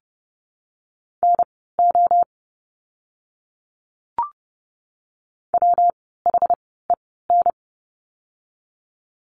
Morse code, heard as a keyed tone, about 30 wpm, spaced as for 17 wpm, 700 Hz. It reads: NO WHEN
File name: course / 30:10wpm / 237 Lesson 38f - Letter q - Words 30wpm